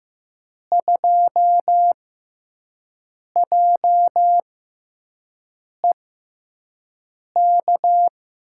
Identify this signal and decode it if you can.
Morse code, heard as a keyed tone, about 15 wpm, spaced as for 5 wpm, 700 Hz. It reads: 2JEK
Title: USA Call Signs - 5 Characters - 6x Character Spacing - 15wpm